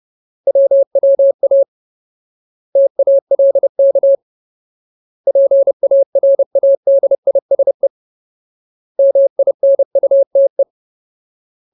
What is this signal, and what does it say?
Morse code, heard as a keyed tone, about 30 wpm, 550 Hz. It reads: WWA TALK PARADISE MINUTE
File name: Ultimate Word List / 30WPM Ultimate